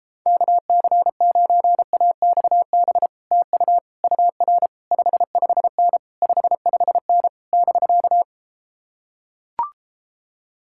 Morse code, heard as a keyed tone, about 33 wpm, 700 Hz. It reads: KC9AXB TU UR 55N 55N <BK>